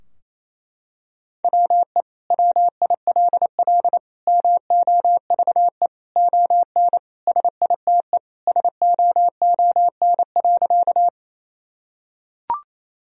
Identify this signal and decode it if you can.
Morse code, heard as a keyed tone, about 28 wpm, 700 Hz. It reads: WE WILL MOVE ON SITE SOON.
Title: Sentences from Top 1000 Words - Review Part 4 - 28wpm